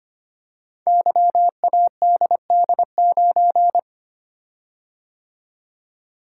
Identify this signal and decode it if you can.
Morse code, heard as a keyed tone, about 25 wpm, 700 Hz. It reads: YADD9
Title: Code Groups - 5 Letter:Numbers - 25wpm